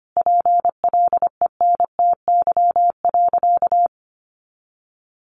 Morse code, heard as a keyed tone, about 25 wpm, 700 Hz. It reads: PLENTY.
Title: Sentences from Top 1300 Words 25wpm